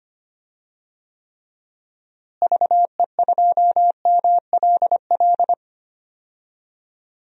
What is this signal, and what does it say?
Morse code, heard as a keyed tone, about 25 wpm, 700 Hz. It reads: VE2MLL
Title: DXCC-100 Call Signs - 6 Characters - Repeated 2x - 25wpm